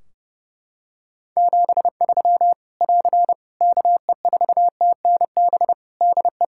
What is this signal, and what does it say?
Morse code, heard as a keyed tone, about 30 wpm, 700 Hz. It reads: 73 <AR> KE4TNB DE
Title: Practice QSO - Exchange 4 - Ending - 30wpm